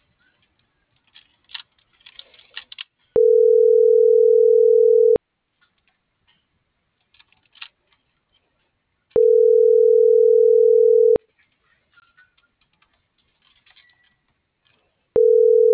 Ringback tone (North America), 440 + 480 Hz.